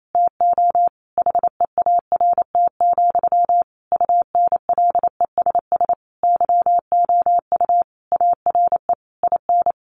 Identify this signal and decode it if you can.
Morse code, heard as a keyed tone, about 28 wpm, 700 Hz. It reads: TO HEART, UNLESS YOU ARE IN
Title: Sentences from Top 1500 Words 28wpm